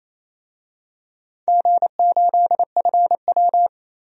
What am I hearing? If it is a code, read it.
Morse code, heard as a keyed tone, about 28 wpm, 700 Hz. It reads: G8FW
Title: DXCC-100 Call Signs - 4 Characters - 28wpm